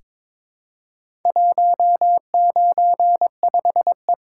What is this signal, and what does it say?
Morse code, heard as a keyed tone, about 22 wpm, 700 Hz. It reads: 195E
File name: practice / 22wpm / Four Letter-Number Combinations - 22wpm